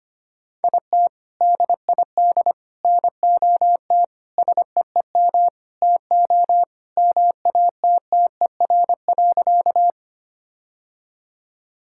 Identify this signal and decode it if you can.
Morse code, heard as a keyed tone, about 25 wpm, 700 Hz. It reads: IT DID NOT SEEM TO MATTER.
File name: practice / 25wpm / Sentences with 4-Word Phrases - 25wpm